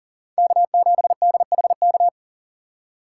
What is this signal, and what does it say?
Morse code, heard as a keyed tone, about 40 wpm, 700 Hz. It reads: K7DHK
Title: USA Call Signs - 5 Characters - 3x Character Spacing - 40wpm